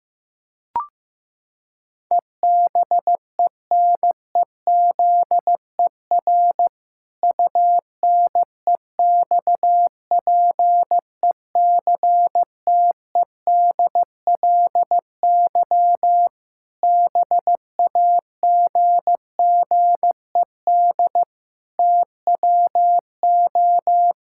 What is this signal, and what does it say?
Morse code, heard as a keyed tone, about 15 wpm, 700 Hz. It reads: EBENEZER UNEXPECTEDLY BAGGED TWO
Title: Pangram Challenge v2 - 15wpm